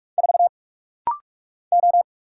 Morse code, heard as a keyed tone, about 44 wpm, 700 Hz. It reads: 4 O
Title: Single Letter-Number - Rapid-Fire 45wpm